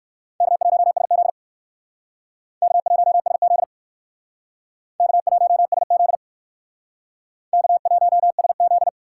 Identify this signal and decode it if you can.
Morse code, heard as a keyed tone, about 58 wpm, 700 Hz. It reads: K1SZ K1SZ K1SZ K1SZ